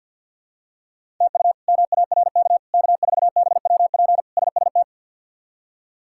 Morse code, heard as a keyed tone, about 51 wpm, 700 Hz. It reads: TU MARK K4BWP SST
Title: K1USN SST QSO Practice - 50wpm